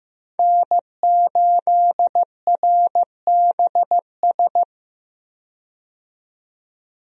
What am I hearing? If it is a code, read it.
Morse code, heard as a keyed tone, about 15 wpm, 700 Hz. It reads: N8RBS
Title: USA Call Signs - 5 Characters - 15wpm